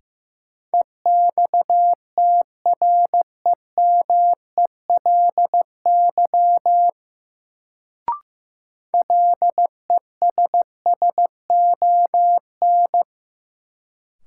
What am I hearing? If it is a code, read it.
Morse code, heard as a keyed tone, about 15 wpm, 700 Hz. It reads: EXTREMELY LESSON
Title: Sentences from Top 1200 Words - New Word Drill Alt - 15wpm